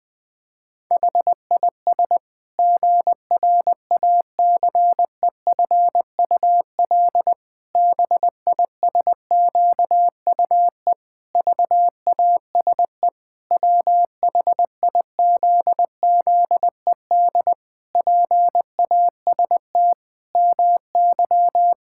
Morse code, heard as a keyed tone, about 20 wpm, 700 Hz. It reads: HIS GRACEFUL BISQUE VASE WHIZZED PAST MY
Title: Pangram Challenge v2 - 20wpm